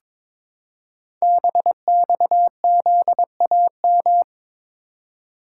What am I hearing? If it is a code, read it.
Morse code, heard as a keyed tone, about 23 wpm, 700 Hz. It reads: BXZAM
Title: Code Groups - 5 Letters - 22wpm